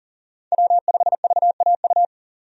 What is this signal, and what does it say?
Morse code, heard as a keyed tone, about 40 wpm, 700 Hz. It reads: W5VAU
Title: USA Call Signs - 5 Characters - 40wpm